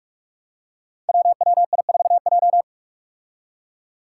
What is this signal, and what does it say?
Morse code, heard as a keyed tone, about 45 wpm, 700 Hz. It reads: WWI4J
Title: Five Challenging Letter-Number Combinations 45wpm